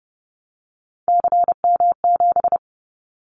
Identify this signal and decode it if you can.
Morse code, heard as a keyed tone, about 30 wpm, 700 Hz. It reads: CM7
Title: Callsign Prefixes for Top 100 DXCC - 30wpm